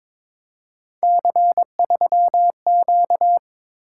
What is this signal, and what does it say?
Morse code, heard as a keyed tone, about 22 wpm, 700 Hz. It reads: C3Q
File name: practice / 22wpm / Three Letter-Number Combinations - 22wpm